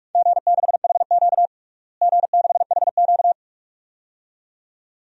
Morse code, heard as a keyed tone, about 46 wpm, 700 Hz. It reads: G6HQ G6HQ
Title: DXCC-100 Call Signs - 4 Characters - Repeated 2x - 45wpm